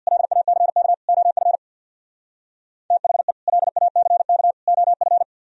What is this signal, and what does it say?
Morse code, heard as a keyed tone, about 62 wpm, 700 Hz. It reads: LACK OF THE LACK OF
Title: Speed-Racing - Sets of 3 Words 50wpm